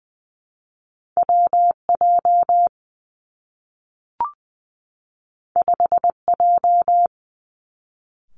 Morse code, heard as a keyed tone, about 20 wpm, 700 Hz. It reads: W J 5 J
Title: Two Challenging Letter-Number Combinations 20wpm